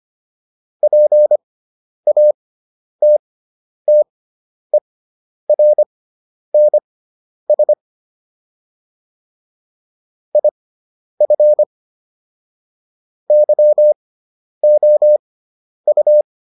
Morse code, heard as a keyed tone, about 25 wpm, spaced as for 10 wpm, 600 Hz. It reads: PATTERNS IF YOU